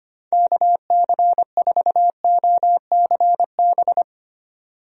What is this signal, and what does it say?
Morse code, heard as a keyed tone, about 25 wpm, 700 Hz. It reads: KC4OCB
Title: USA Call Signs - 6 Characters - 25wpm